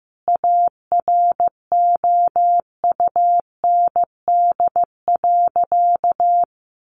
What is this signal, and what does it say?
Morse code, heard as a keyed tone, about 15 wpm, 700 Hz. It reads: AROUND.
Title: Sentences from Top 1300 Words 15wpm